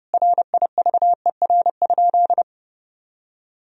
Morse code, heard as a keyed tone, about 30 wpm, 700 Hz. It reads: RIVER?
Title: Sentences from Top 1000 Words - Review Part 4 - 30wpm